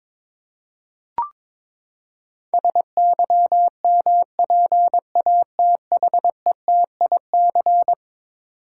Morse code, heard as a keyed tone, about 22 wpm, 700 Hz. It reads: SYMPATHETIC
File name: practice / 22wpm / Eleven Letter Words - 22wpm